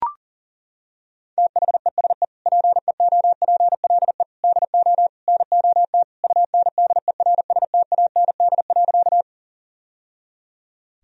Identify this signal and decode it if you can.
Morse code, heard as a keyed tone, about 40 wpm, 700 Hz. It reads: THESE PEOPLE DO NOT UNDERSTAND.
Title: Sentences from Top 1000 Words - Review Part 1 - 40wpm